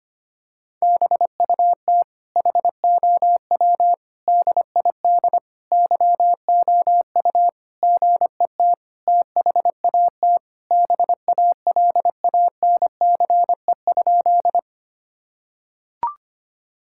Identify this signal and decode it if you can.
Morse code, heard as a keyed tone, about 25 wpm, 700 Hz. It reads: BUT HOW DID YOU GET THAT BALANCE?
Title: Sentences from Top 1000 Words - Review Part 6 - 25wpm